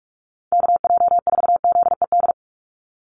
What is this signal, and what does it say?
Morse code, heard as a keyed tone, about 44 wpm, 700 Hz. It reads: KJ4ZED
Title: USA Call Signs 45wpm